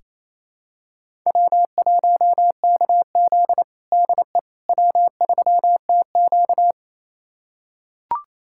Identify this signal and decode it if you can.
Morse code, heard as a keyed tone, about 28 wpm, 700 Hz. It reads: W1KZ DE W3TQ
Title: USA Callsign DE Callsign - 4 Characters - 28wpm